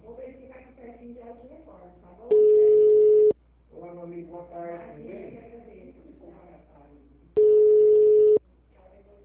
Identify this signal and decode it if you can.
Ringback tone (Europe, 425 Hz), with speech.
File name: out- -2007- - - .465